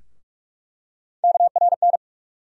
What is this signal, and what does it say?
Morse code, heard as a keyed tone, about 44 wpm, 700 Hz. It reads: KRN
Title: USA Call Sign Postfixes - 3 Letters 45wpm